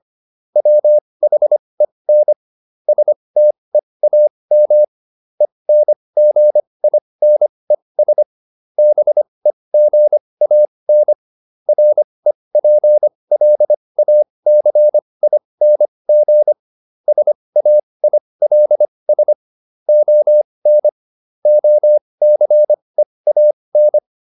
Morse code, heard as a keyed tone, about 25 wpm, spaced as for 20 wpm, 600 Hz. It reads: WHEN STEAM ENGINES BEGAN REPLACING SAILS ON OCEAN